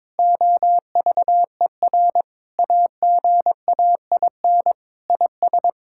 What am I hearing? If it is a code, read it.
Morse code, heard as a keyed tone, about 21 wpm, 700 Hz. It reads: OVER AGAIN IS
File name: Sentences with 7-Word Phrases - 22wpm